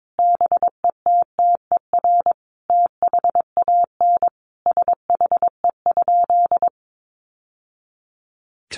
Morse code, heard as a keyed tone, about 22 wpm, 700 Hz. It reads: BETTER THAN SHE?
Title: Sentences from Top 1200 Words 22wpm